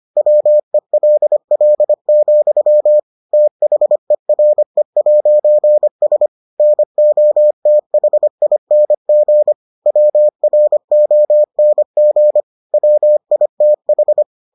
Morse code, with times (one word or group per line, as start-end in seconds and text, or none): 0.2-3.0 WELL,
3.3-6.3 THERE'S
6.6-9.5 NOTHING
9.9-12.4 WRONG
12.7-14.2 WITH